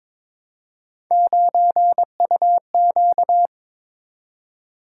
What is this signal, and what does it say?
Morse code, heard as a keyed tone, about 22 wpm, 700 Hz. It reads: 9UQ
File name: practice / 22wpm / Three Letter-Number Combinations - 22wpm